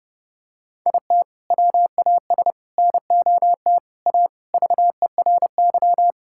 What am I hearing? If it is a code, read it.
Morse code, heard as a keyed tone, about 32 wpm, 700 Hz. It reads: IT WAS NOT A VERY
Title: Sentences from Top 1000 Words - Review Part 6 - 30wpm